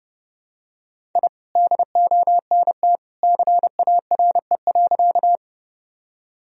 Morse code, heard as a keyed tone, about 30 wpm, 700 Hz. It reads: I DONT CARE.
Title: Sentences from Top 1000 Words - Review Part 1 - 30wpm